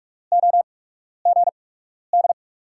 Morse code, heard as a keyed tone, about 47 wpm, spaced as for 12 wpm, 700 Hz. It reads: OGD
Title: USA Call Signs - 5 Characters - 8x Character Spacing - 45wpm